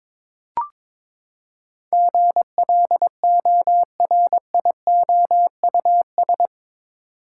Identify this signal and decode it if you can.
Morse code, heard as a keyed tone, about 22 wpm, 700 Hz. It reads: GLORIOUS
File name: practice / 22wpm / Eight Letter Words - 22wpm